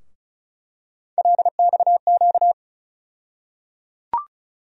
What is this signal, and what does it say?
Morse code, heard as a keyed tone, about 34 wpm, 700 Hz. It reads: LXQ
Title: USA Call Sign Postfixes - 3 Letters 35wpm